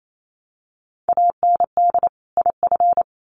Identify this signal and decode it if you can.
Morse code, heard as a keyed tone, about 28 wpm, 700 Hz. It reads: AND IF
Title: Sets of 2 Words 28wpm